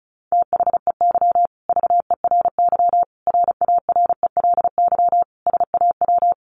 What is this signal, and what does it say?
Morse code, heard as a keyed tone, about 34 wpm, 700 Hz. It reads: THEY VERY RARELY SAW
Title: Sentences from Top 1400 Words 35wpm